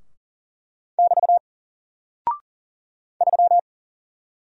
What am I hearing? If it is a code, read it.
Morse code, heard as a keyed tone, about 40 wpm, 700 Hz. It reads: <BT> 3
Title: Single Character (Letter:Number:Punctuation:Prosign) Alt 40wpm